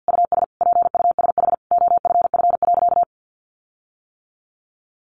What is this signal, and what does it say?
Morse code, heard as a keyed tone, about 50 wpm, 700 Hz. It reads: US PUSH OFF.
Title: Sentences from Top 1300 Words 50wpm